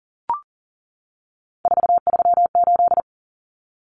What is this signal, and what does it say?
Morse code, heard as a keyed tone, about 40 wpm, 700 Hz. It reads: 438
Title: Common Cut Numbers - 3-Digit 40wpm